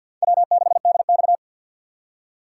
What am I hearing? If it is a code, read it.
Morse code, heard as a keyed tone, about 53 wpm, 700 Hz. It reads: W6DX
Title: USA Call Signs - 4 Characters - 6x Character Spacing - 50wpm